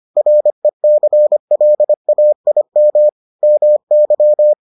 Morse code, heard as a keyed tone, about 25 wpm, 600 Hz. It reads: RECLAIM MY